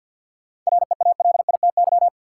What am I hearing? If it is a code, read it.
Morse code, heard as a keyed tone, about 53 wpm, 700 Hz. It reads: REALITY
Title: Sentences from Top 1200 Words - New Word Drill - 50wpm